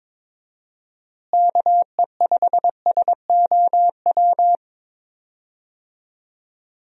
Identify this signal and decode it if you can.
Morse code, heard as a keyed tone, about 23 wpm, 700 Hz. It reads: KE5SOW
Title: USA Call Signs - 6 Characters - Repeated 3x - 22wpm